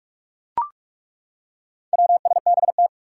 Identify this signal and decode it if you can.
Morse code, heard as a keyed tone, about 42 wpm, 700 Hz. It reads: WSBT
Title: Four Challenging Letter-Number Combinations 45wpm